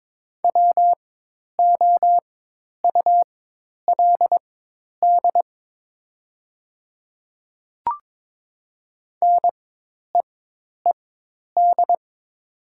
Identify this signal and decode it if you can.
Morse code, heard as a keyed tone, about 22 wpm, spaced as for 10 wpm, 700 Hz. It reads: WOULD NEED
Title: 4x Character Spacing - Top 200 Words - 22wpm